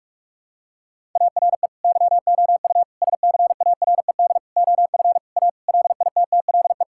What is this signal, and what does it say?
Morse code, heard as a keyed tone, about 46 wpm, 700 Hz. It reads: ARE YOU SCARED OF A LITTLE
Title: Sentences from Top 1000 Words - Review Part 5 - 45wpm